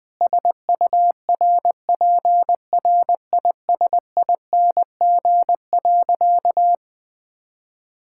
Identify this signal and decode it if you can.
Morse code, heard as a keyed tone, about 20 wpm, 700 Hz. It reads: SURPRISING.
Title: Sentences with 4-Word Phrases - 20wpm